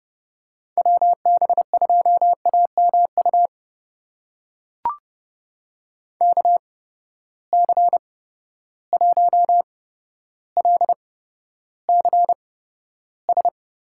Morse code, heard as a keyed tone, about 30 wpm, spaced as for 13 wpm, 700 Hz. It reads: WB2AMU KC1LCS
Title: USA Call Signs - 6 Characters - 8x Character Spacing - 30wpm